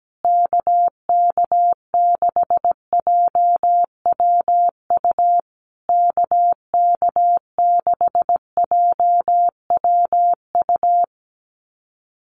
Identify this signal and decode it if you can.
Morse code, heard as a keyed tone, about 17 wpm, 700 Hz. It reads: KK6JWU KK6JWU
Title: USA Call Signs Repeated 3x 17wpm